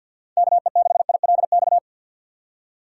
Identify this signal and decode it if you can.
Morse code, heard as a keyed tone, about 48 wpm, 700 Hz. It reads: KE6ILX
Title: USA Call Signs - 6 Characters - 3x Character Spacing - 50wpm